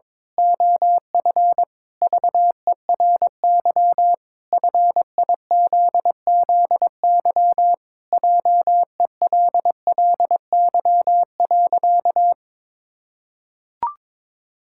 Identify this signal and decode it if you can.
Morse code, heard as a keyed tone, about 22 wpm, 700 Hz. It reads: OF VERY FIZZY JELLY.
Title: Pangram Challenge v2 - 22wpm